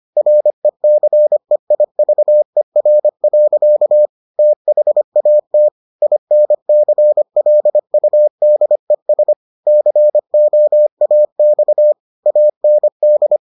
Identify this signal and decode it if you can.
Morse code, heard as a keyed tone, about 25 wpm, 600 Hz. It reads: RECEIVER. THAT INCLUDES COAX AND